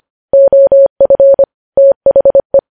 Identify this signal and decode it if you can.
Morse code, heard as a keyed tone, about 25 wpm, 550 Hz. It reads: OF THE
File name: podcast / News headlines-1x-25x25